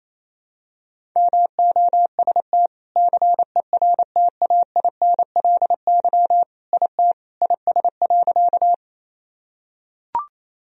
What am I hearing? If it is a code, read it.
Morse code, heard as a keyed tone, about 28 wpm, 700 Hz. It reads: MOST CERTAINLY IT IS.
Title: Sentences from Top 1000 Words - Review Part 1 - 28wpm